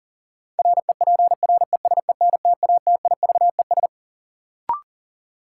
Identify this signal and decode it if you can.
Morse code, heard as a keyed tone, about 40 wpm, 700 Hz. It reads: REPRESENTATIVES
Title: Sets of 5 Words Encore - New Word Drill - 40wpm